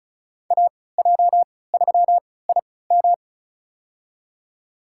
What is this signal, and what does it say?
Morse code, heard as a keyed tone, about 34 wpm, spaced as for 20 wpm, 700 Hz. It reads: AJ3IM
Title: USA Call Signs - 5 Characters - 3x Character Spacing - 35wpm